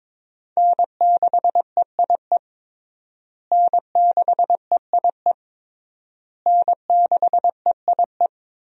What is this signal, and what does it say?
Morse code, heard as a keyed tone, about 22 wpm, 700 Hz. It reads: N6EIE N6EIE N6EIE
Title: USA Call Signs - 5 Characters - Repeated 3x - 22wpm